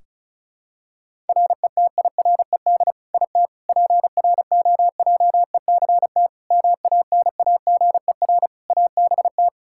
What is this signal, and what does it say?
Morse code, heard as a keyed tone, about 34 wpm, 700 Hz. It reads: RETIRED IT PROJECT MANAGER ABT
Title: Practice QSO - Exchange 3 - More Info - 35wpm